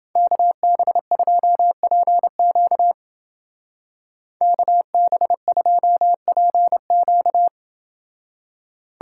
Morse code, heard as a keyed tone, about 28 wpm, 700 Hz. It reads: KB2PQ KB2PQ